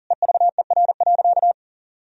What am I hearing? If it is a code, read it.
Morse code, heard as a keyed tone, about 40 wpm, 700 Hz. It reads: EVER.
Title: Sentences from Top 1000 Words - Review Part 2 - 40wpm